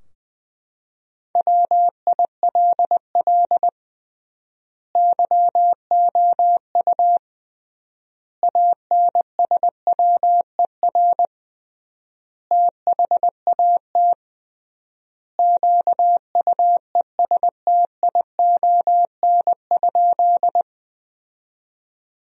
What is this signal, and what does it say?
Morse code, heard as a keyed tone, about 21 wpm, 700 Hz. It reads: WILL YOU ANSWER THAT QUESTION?
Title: 3x Word-Spacing - Sentences from Top 400 Words - 20wpm